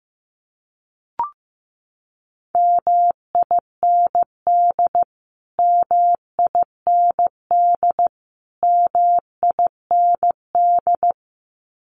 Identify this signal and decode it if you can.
Morse code, heard as a keyed tone, about 15 wpm, 700 Hz. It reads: MIND MIND MIND
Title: Top 500 Words Repeated 15wpm